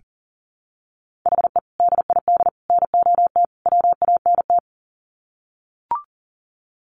Morse code, heard as a keyed tone, about 40 wpm, 700 Hz. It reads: HE DID NOT WANT
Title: Sets of 4 Words 40wpm